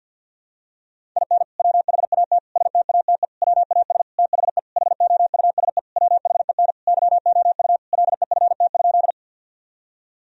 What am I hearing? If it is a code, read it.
Morse code, heard as a keyed tone, about 50 wpm, 700 Hz. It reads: IN WHAT STATE WAS THE HOUSE WHEN YOU LEFT?
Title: Sentences from Top 1000 Words - Review Part 2 - 50wpm